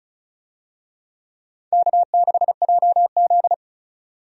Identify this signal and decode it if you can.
Morse code, heard as a keyed tone, about 36 wpm, 700 Hz. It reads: K6JZ
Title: USA Call Signs - 4 Characters - Repeated 2x - 35wpm